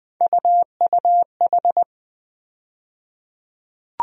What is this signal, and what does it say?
Morse code, heard as a keyed tone, about 20 wpm, 700 Hz. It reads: UUH
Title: USA Call Sign Postfixes - 3 Letters 20wpm